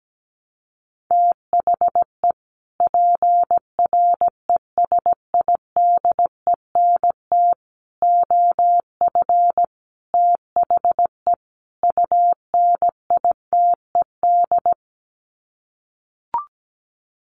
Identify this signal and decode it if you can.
Morse code, heard as a keyed tone, about 17 wpm, 700 Hz. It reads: THE PRESIDENT OF THE UNITED
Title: Sets of 5 Words 17wpm